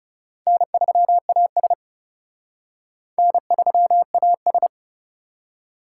Morse code, heard as a keyed tone, about 36 wpm, 700 Hz. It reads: N3AS N3AS
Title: Speed Racing - USA 1x2 Call Signs 30wpm